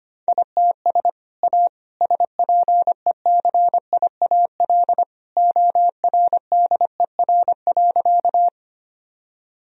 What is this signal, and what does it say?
Morse code, heard as a keyed tone, about 25 wpm, 700 Hz. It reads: ITS A SPECIAL ORDER.
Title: Sentences from Top 1000 Words - Review Part 3 - 25wpm